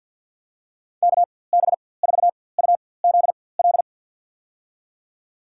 Morse code, heard as a keyed tone, about 53 wpm, spaced as for 24 wpm, 700 Hz. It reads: KB4UZL
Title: USA Call Signs - 6 Characters - 4x Character Spacing - 50wpm